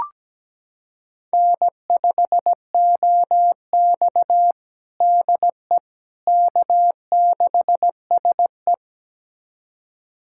Morse code, heard as a keyed tone, about 17 wpm, 700 Hz. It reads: N5OX DE K6SE
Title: USA Callsign DE Callsign - 4 Characters - 17wpm